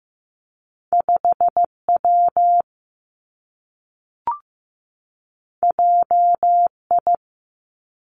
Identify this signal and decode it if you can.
Morse code, heard as a keyed tone, about 15 wpm, 700 Hz. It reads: 5 W J I
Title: Two Challenging Letter-Number Combinations 15wpm